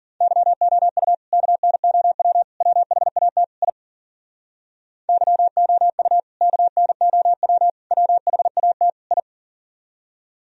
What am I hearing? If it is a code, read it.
Morse code, heard as a keyed tone, about 51 wpm, 700 Hz. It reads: YOU KNOW WHAT I YOU KNOW WHAT I